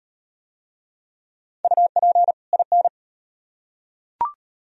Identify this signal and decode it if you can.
Morse code, heard as a keyed tone, about 35 wpm, 700 Hz. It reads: UP IN